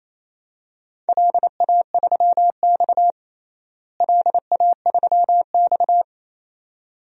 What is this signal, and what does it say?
Morse code, heard as a keyed tone, about 28 wpm, 700 Hz. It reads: LA3X LA3X
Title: DXCC-100 Call Signs - 4 Characters - Repeated 2x - 28wpm